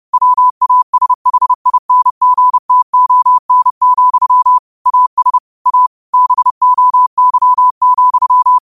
Morse code, heard as a keyed tone, about 31 wpm, 1000 Hz. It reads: WASHINGTON, AS A BOY,
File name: qotdfiles / 30wpm